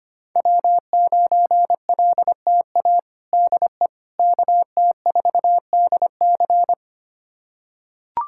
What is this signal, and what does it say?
Morse code, heard as a keyed tone, about 25 wpm, 700 Hz. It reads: W9LTA DE KT4DC
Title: USA Callsign DE Callsign - 5 Characters - 25wpm